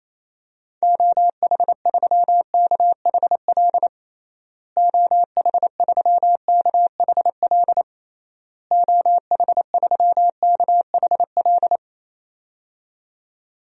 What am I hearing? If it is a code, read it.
Morse code, heard as a keyed tone, about 28 wpm, 700 Hz. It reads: OH3KHL OH3KHL OH3KHL
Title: DXCC-100 Call Signs - 6 Characters - Repeated 3x - 28wpm